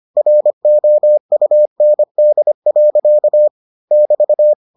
Morse code, heard as a keyed tone, about 25 wpm, 600 Hz. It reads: ROUND. <BT>